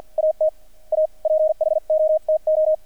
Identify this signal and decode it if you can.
Morse code, heard as a keyed tone, about 50 wpm, spaced as for 43 wpm, 620 Hz. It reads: AT A PHOTO
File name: archived / CRSnet